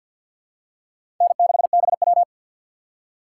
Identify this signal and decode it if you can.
Morse code, heard as a keyed tone, about 46 wpm, 700 Hz. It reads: N6BW
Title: USA Call Signs - 4 Characters - 6x Character Spacing - 50wpm